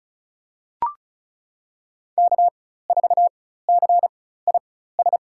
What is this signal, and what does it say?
Morse code, heard as a keyed tone, about 36 wpm, spaced as for 16 wpm, 700 Hz. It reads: K4CIS
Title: USA Call Signs - 5 Characters - 4x Character Spacing - 35wpm